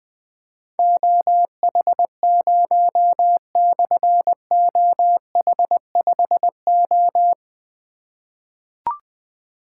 Morse code, heard as a keyed tone, about 20 wpm, 700 Hz. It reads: OH0/OH5O
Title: Difficult Call Signs 20wpm